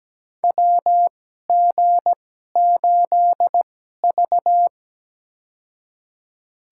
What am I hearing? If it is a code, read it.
Morse code, heard as a keyed tone, about 17 wpm, spaced as for 12 wpm, 700 Hz. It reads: WG8V
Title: USA Call Signs - 2x Character Spacing - 17wpm